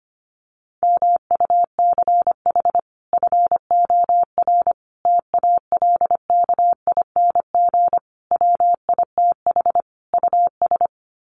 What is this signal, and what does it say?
Morse code, heard as a keyed tone, about 25 wpm, 700 Hz. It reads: MUCH FOR TALKING WITH US
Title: Sets of 5 Words Encore 25wpm